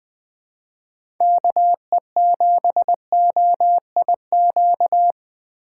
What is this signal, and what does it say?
Morse code, heard as a keyed tone, about 20 wpm, 700 Hz. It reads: KE7OIQ
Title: USA Call Signs - 6 Characters - 6x Character Spacing - 20wpm